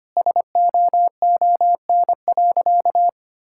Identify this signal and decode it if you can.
Morse code, heard as a keyed tone, about 25 wpm, 700 Hz. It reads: SOON.
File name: practice / 25wpm / Sentences with 3-Word Phrases - 25wpm